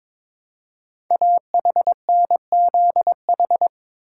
Morse code, heard as a keyed tone, about 22 wpm, 700 Hz. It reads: AHNZH
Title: Code Groups - 5 Letters - 22wpm